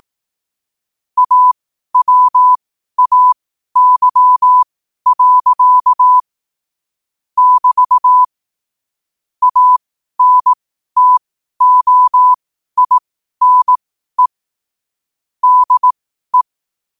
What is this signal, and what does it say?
Morse code, heard as a keyed tone, about 18 wpm, spaced as for 12 wpm, 1000 Hz. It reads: AWAY. <BT> ANTOINE DE
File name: qotdfiles / 12wpm